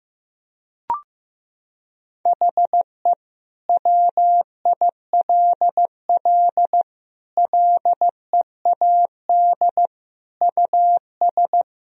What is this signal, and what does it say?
Morse code, heard as a keyed tone, about 15 wpm, 700 Hz. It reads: HE WILL LEAD US
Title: Sentences from Top 1000 Words - Review Part 2 - 15wpm